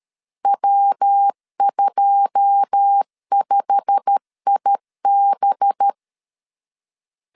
Morse code, heard as a keyed tone, about 13 wpm, 800 Hz. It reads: W25IB
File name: CW / DET / 1 9 1500C S C 12wpm 3811